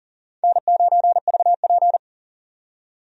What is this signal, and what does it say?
Morse code, heard as a keyed tone, about 40 wpm, 700 Hz. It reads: N9VP
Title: USA Call Signs - 4 Characters - 40wpm